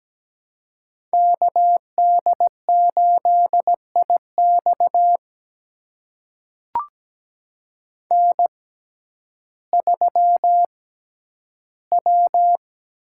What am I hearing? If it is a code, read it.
Morse code, heard as a keyed tone, about 17 wpm, spaced as for 10 wpm, 700 Hz. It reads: KD8IX N3W
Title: USA Call Signs - 5 Characters - 6x Character Spacing - 17wpm